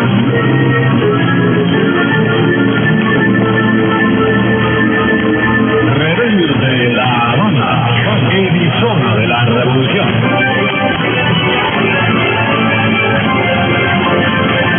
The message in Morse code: none

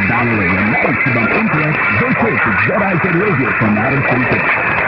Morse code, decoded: none

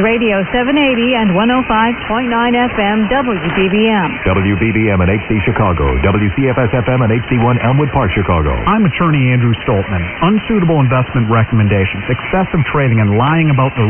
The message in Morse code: none

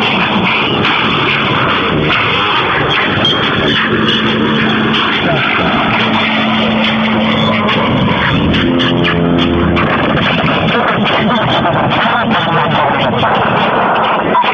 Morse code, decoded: none

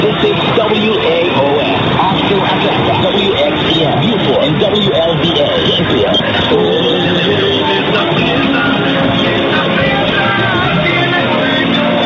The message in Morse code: none

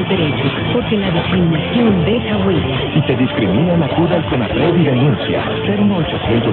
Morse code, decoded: none

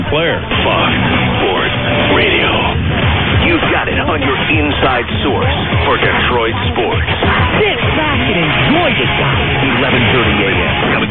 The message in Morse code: none

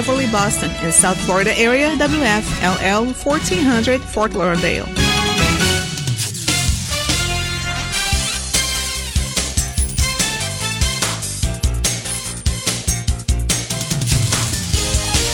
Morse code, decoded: none

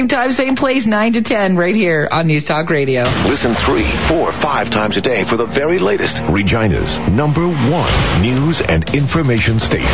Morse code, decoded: none

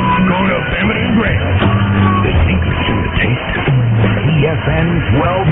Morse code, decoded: none

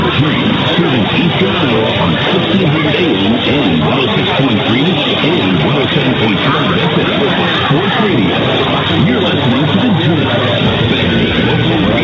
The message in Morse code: none

